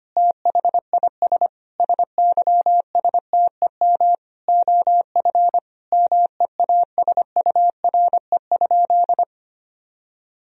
Morse code, heard as a keyed tone, about 25 wpm, 700 Hz. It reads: THIS SYSTEM OF MEASURE?
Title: Sentences from Top 1000 Words - Review Part 3 - 25wpm